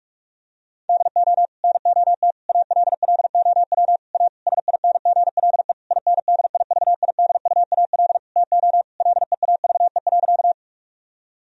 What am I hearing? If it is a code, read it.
Morse code, heard as a keyed tone, about 46 wpm, 700 Hz. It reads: DO NOT ALLOW A SINGLE INDIVIDUAL TO LEAVE.
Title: Sentences from Top 1000 Words - Review Part 4 - 45wpm